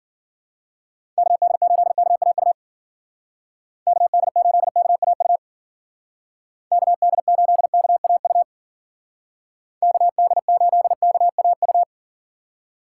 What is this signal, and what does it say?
Morse code, heard as a keyed tone, about 50 wpm, 700 Hz. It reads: KD8KAU KD8KAU KD8KAU KD8KAU
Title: Speed Racing - USA 2x3 Call Signs 40wpm